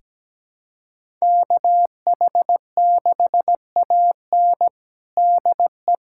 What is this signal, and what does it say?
Morse code, heard as a keyed tone, about 17 wpm, 700 Hz. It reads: KH6AN DE
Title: USA Callsign DE Callsign 17wpm